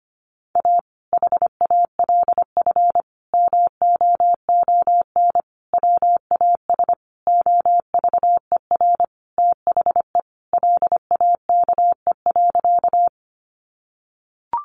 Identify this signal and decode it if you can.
Morse code, heard as a keyed tone, about 25 wpm, 700 Hz. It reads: A HALF MOON WAS OVER THE LAKE.
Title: Sentences from Top 1100 Words 25wpm